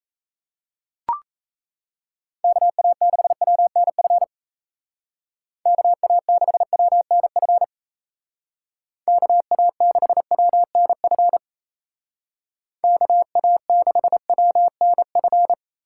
Morse code, heard as a keyed tone, about 34 wpm, 700 Hz. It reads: KA6WNF KA6WNF KA6WNF KA6WNF
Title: Speed Racing - USA 2x3 Call Signs 28wpm